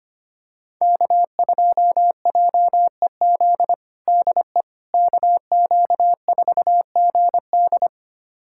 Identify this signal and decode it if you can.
Morse code, heard as a keyed tone, about 25 wpm, 700 Hz. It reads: K2JEZ DE KQ4GD
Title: USA Callsign DE Callsign - 5 Characters - 25wpm